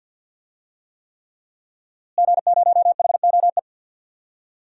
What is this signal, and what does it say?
Morse code, heard as a keyed tone, about 51 wpm, 700 Hz. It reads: G0HOE
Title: DXCC-100 Call Signs - 5 Characters - Repeated 3x - 50wpm